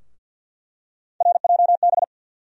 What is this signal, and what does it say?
Morse code, heard as a keyed tone, about 48 wpm, 700 Hz. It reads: RJB
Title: USA Call Sign Postfixes - 3 Letters 50wpm